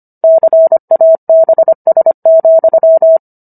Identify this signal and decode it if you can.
Morse code, heard as a keyed tone, about 25 wpm, 650 Hz. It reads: CABS,